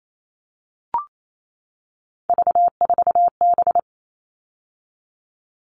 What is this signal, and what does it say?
Morse code, heard as a keyed tone, about 28 wpm, 700 Hz. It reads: V4B
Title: All Cut Numbers - 3-Digit 28wpm